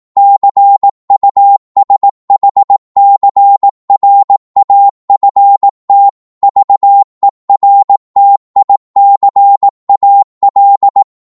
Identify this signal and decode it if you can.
Morse code, heard as a keyed tone, about 18 wpm, 800 Hz. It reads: CUSHCRAFT VERTICAL
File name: Morse / qso-18wpm